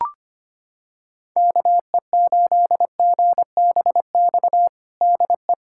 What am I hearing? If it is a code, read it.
Morse code, heard as a keyed tone, about 25 wpm, 700 Hz. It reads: KE8GBX DE
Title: USA Callsign DE Callsign - 6 Characters - 25wpm